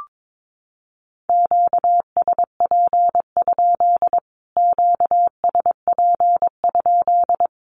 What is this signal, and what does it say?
Morse code, heard as a keyed tone, about 22 wpm, 700 Hz. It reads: QSP? QSP?
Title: Q-Codes Repeated 3x 22wpm